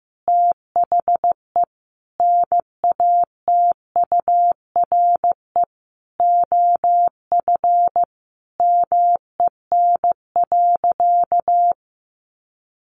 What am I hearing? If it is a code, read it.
Morse code, heard as a keyed tone, about 15 wpm, 700 Hz. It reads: THE NATURE OF MEN.
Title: Sentences from Top 1500 Words 15wpm